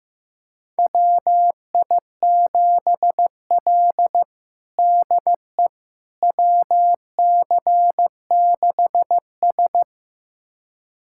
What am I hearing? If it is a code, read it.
Morse code, heard as a keyed tone, about 15 wpm, 700 Hz. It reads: WI7L DE WC6S
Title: USA Callsign DE Callsign - 4 Characters - 15wpm